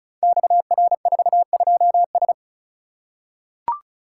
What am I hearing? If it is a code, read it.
Morse code, heard as a keyed tone, about 36 wpm, 700 Hz. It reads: XR42S
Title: Code Groups - 5 Letter:Numbers - 35wpm